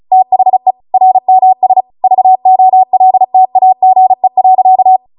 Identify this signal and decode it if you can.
Morse code, heard as a keyed tone, about 36 wpm, 750 Hz. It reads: THE RMS VOLTAGE.